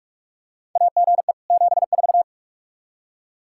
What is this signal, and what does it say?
Morse code, heard as a keyed tone, about 46 wpm, 700 Hz. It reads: AGE 74
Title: Practice QSO - Exchange 3 - More Info - 45wpm